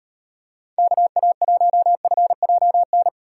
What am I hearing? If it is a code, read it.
Morse code, heard as a keyed tone, about 38 wpm, 700 Hz. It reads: KA1FJN